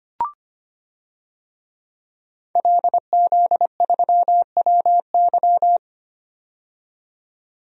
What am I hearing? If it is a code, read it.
Morse code, heard as a keyed tone, about 25 wpm, 700 Hz. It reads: LZ3WY
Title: DXCC-100 Call Signs - 5 Characters - 25wpm